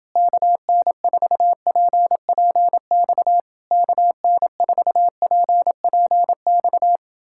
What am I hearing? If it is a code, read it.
Morse code, heard as a keyed tone, about 30 wpm, 700 Hz. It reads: KN4PPX KN4PPX